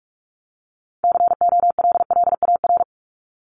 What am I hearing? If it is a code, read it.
Morse code, heard as a keyed tone, about 44 wpm, 700 Hz. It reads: COLLAR
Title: Top 1751-2000 Words 45wpm